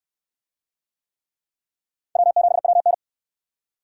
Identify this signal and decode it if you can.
Morse code, heard as a keyed tone, about 68 wpm, 700 Hz. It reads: W7WN